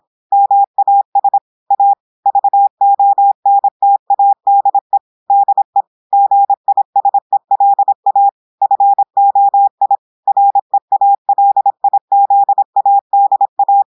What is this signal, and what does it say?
Morse code, heard as a keyed tone, about 26 wpm, 800 Hz. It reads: MAS A VONTADE DE GISELA FOI REALIZADA
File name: audios / portugais / portugais texte en clair 1 vitesse 26 mots minutes